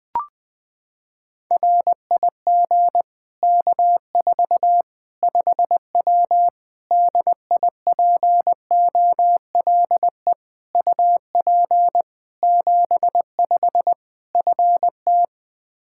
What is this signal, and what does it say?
Morse code, heard as a keyed tone, about 20 wpm, 700 Hz. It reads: RIG K4 5W DIPOLE UP 75 FT
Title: Practice QSO - Exchange 2 - Info - 20wpm